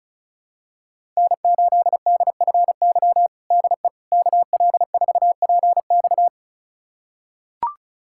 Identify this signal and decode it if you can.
Morse code, heard as a keyed tone, about 34 wpm, 690 Hz. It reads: N8DFY DE KL4PX
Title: USA Callsign DE Callsign - 5 Characters - 35wpm